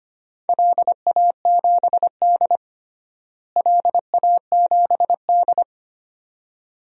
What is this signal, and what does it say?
Morse code, heard as a keyed tone, about 25 wpm, 700 Hz. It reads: LA7D LA7D
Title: DXCC-100 Call Signs - 4 Characters - Repeated 2x - 25wpm